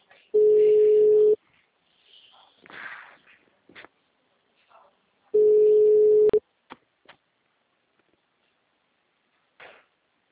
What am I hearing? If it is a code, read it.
Ringback tone (Europe, 425 Hz).